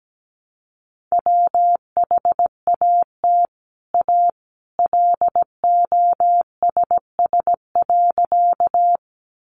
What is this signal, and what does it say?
Morse code, heard as a keyed tone, about 17 wpm, 700 Hz. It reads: WHAT A LOSS.
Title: Sentences from Top 1400 Words 17wpm